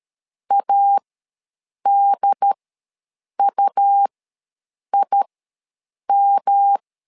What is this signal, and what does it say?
Morse code, heard as a keyed tone, about 12 wpm, spaced as for 7 wpm, 800 Hz. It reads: ADUIM
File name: LEZIONI / 3 MWRO / 3 M ws 12wpm 1745